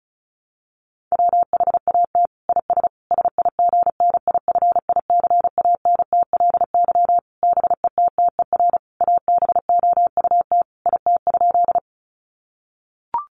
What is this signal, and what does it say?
Morse code, heard as a keyed tone, about 34 wpm, 700 Hz. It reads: WHAT IS SIGNIFICANTLY BETTER ABOUT IT?
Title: Sentences from Top 1500 Words 35wpm